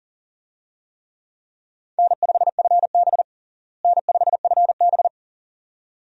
Morse code, heard as a keyed tone, about 40 wpm, 700 Hz. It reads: N5FB N5FB
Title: USA Call Signs - 4 Characters - Repeated 2x - 40wpm